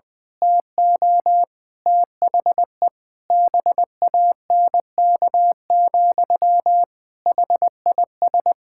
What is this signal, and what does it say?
Morse code, heard as a keyed tone, about 20 wpm, 700 Hz. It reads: TO THE BANK, HIS
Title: Pangram Challenge v2 - 20wpm